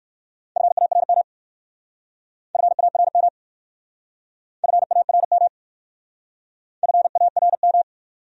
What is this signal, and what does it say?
Morse code, heard as a keyed tone, about 54 wpm, 700 Hz. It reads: FARM FARM FARM FARM
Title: Speed-Racing Top 500 Words 45wpm